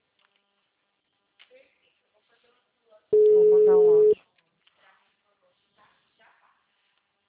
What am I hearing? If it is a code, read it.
Ringback tone (Europe, 425 Hz), with speech.